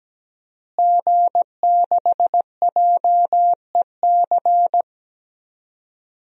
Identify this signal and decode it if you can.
Morse code, heard as a keyed tone, about 17 wpm, 700 Hz. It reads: G6JEC
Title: DXCC-100 Call Signs - 5 Characters - Repeated 2x - 17wpm